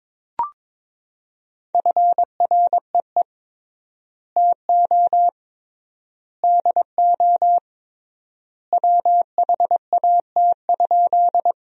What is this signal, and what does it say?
Morse code, heard as a keyed tone, about 22 wpm, 700 Hz. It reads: FREE TO DO WHAT?
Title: 3x Word-Spacing - Sentences from Top 500 Words - 22wpm